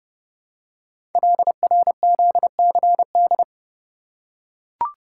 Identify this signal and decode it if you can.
Morse code, heard as a keyed tone, about 28 wpm, 700 Hz. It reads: LRZCD
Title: Code Groups - 5 Letters - 30wpm